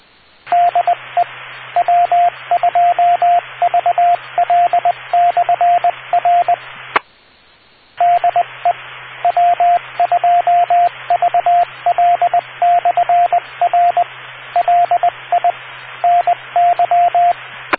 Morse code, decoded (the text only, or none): DE W2VL/R DE W2VL/R LI NY